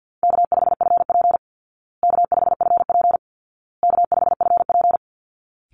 Morse code, heard as a keyed tone, about 50 wpm, 700 Hz. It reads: K5FP K5FP K5FP
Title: Easy Call Signs Repeated 3x - 50wpm